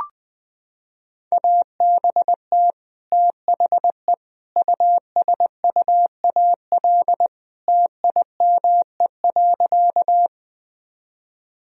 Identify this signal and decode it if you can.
Morse code, heard as a keyed tone, about 20 wpm, 700 Hz. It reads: ABT THE USUAL TIME.
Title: Sentences from Top 400 Words and Abbreviations 20wpm